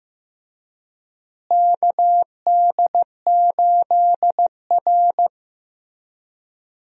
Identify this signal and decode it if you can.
Morse code, heard as a keyed tone, about 15 wpm, 700 Hz. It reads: KD8R
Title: USA Call Signs - 4 Characters - Repeated 3x - 15wpm